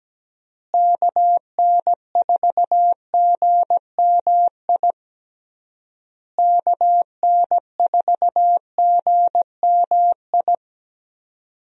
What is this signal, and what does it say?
Morse code, heard as a keyed tone, about 17 wpm, 700 Hz. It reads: KN4GMI KN4GMI
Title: USA Call Signs - 6 Characters - Repeated 3x - 17wpm